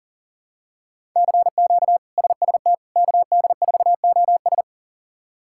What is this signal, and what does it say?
Morse code, heard as a keyed tone, about 40 wpm, 700 Hz. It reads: CQ SST KD4OS
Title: K1USN SST QSO Practice - 40wpm